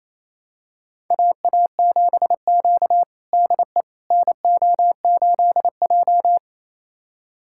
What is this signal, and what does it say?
Morse code, heard as a keyed tone, about 28 wpm, 700 Hz. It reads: AA7Q DE NO8J
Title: USA Callsign DE Callsign - 4 Characters - 28wpm